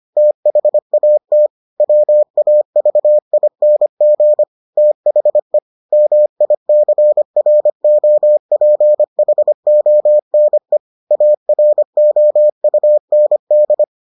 Morse code, heard as a keyed tone, about 25 wpm, 600 Hz. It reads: THAT WAVING THE MICROPHONE AROUND